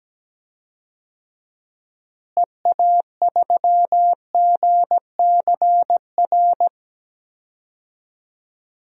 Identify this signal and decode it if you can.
Morse code, heard as a keyed tone, about 17 wpm, 700 Hz. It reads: EA3GCR
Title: DXCC-100 Call Signs - 6 Characters - Repeated 3x - 17wpm